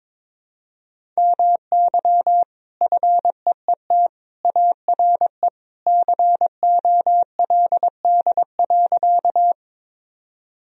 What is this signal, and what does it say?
Morse code, heard as a keyed tone, about 22 wpm, 700 Hz. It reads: MY FEET ARE COLD.
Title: Sentences from Top 1000 Words - Review Part 1 - 22wpm